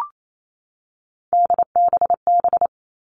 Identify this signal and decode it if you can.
Morse code, heard as a keyed tone, about 28 wpm, 700 Hz. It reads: DBB
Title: Three Challenging Letter-Number Combinations 28wpm